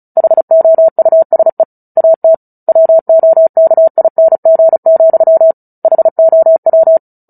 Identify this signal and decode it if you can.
Morse code, heard as a keyed tone, about 34 wpm, 650 Hz. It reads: HOUSE AT WOKING, HOW
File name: Book Two / 09-Chapter 9 - 35 WPM